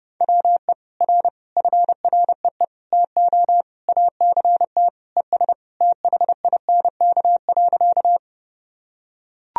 Morse code, heard as a keyed tone, about 30 wpm, 700 Hz. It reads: WE R FREE TO ACT ES THINK.
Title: Sentences from Top 500 Words and Abbreviations 30wpm